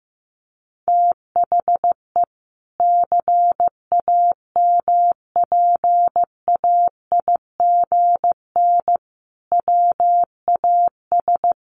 Morse code, heard as a keyed tone, about 15 wpm, 700 Hz. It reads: THE CAMPAIGN WAS
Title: Sentences from Top 1200 Words 15wpm